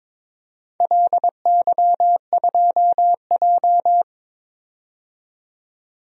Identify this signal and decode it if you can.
Morse code, heard as a keyed tone, about 21 wpm, 700 Hz. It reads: LY2J
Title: DXCC-100 Call Signs - 4 Characters - 22wpm